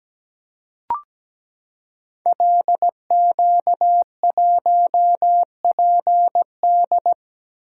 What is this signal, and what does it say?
Morse code, heard as a keyed tone, about 17 wpm, 700 Hz. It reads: LQ1PD
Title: Code Groups - 5 Letter:Numbers - 17wpm